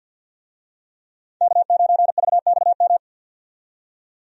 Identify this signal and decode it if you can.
Morse code, heard as a keyed tone, about 47 wpm, 700 Hz. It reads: K9VXM
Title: USA Call Signs - 5 Characters - 50wpm